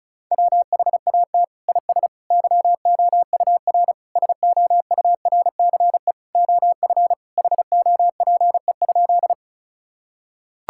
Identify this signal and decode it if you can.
Morse code, heard as a keyed tone, about 36 wpm, 700 Hz. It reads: WHAT IS YOUR SOURCE OF HOPE?
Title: Sentences from Top 1000 Words - Review Part 5 - 35wpm